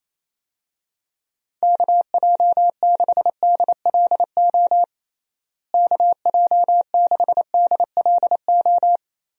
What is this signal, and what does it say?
Morse code, heard as a keyed tone, about 28 wpm, 700 Hz. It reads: KJ6DLO KJ6DLO
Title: USA Call Signs - 6 Characters - Repeated 3x - 28wpm